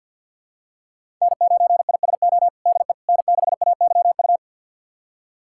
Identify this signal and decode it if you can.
Morse code, heard as a keyed tone, about 52 wpm, 690 Hz. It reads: N9ISO DE N6AYU
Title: USA Callsign DE Callsign - 5 Characters - 50wpm